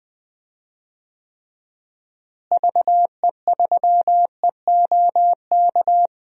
Morse code, heard as a keyed tone, about 20 wpm, 700 Hz. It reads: VE3EOK
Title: DXCC-100 Call Signs - 6 Characters - Repeated 3x - 20wpm